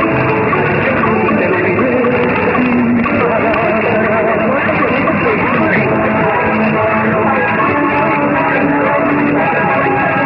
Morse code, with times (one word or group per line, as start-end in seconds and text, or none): none